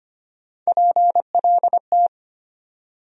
Morse code, heard as a keyed tone, about 25 wpm, 700 Hz. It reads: PLT
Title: USA Call Sign Postfixes - 3 Letters 25wpm